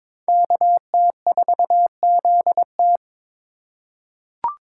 Morse code, heard as a keyed tone, about 22 wpm, 700 Hz. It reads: KT4ZT
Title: USA Call Signs - 5 Characters - 4x Character Spacing - 22wpm